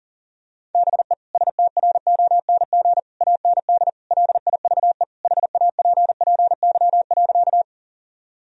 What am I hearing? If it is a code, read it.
Morse code, heard as a keyed tone, about 40 wpm, 700 Hz. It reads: BE STRONG AND LIVE HAPPY.
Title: Sentences from Top 1000 Words - Review Part 3 - 40wpm